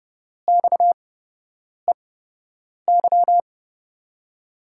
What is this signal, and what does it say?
Morse code, heard as a keyed tone, about 29 wpm, spaced as for 8 wpm, 700 Hz. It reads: XEY
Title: USA Call Signs - 5 Characters - 8x Character Spacing - 30wpm